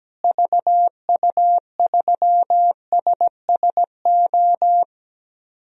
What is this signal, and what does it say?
Morse code, heard as a keyed tone, about 17 wpm, 700 Hz. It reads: VU3SSO
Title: DXCC-100 Call Signs - 6 Characters - Repeated 2x - 17wpm